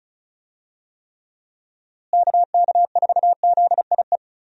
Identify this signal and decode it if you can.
Morse code, heard as a keyed tone, about 36 wpm, 700 Hz. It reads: KK4ZIE
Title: USA Call Signs - 6 Characters - Repeated 3x - 35wpm